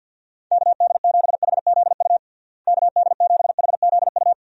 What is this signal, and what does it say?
Morse code, heard as a keyed tone, about 52 wpm, 700 Hz. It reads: KD7HZU KD7HZU
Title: USA Call Signs - 6 Characters - Repeated 3x - 50wpm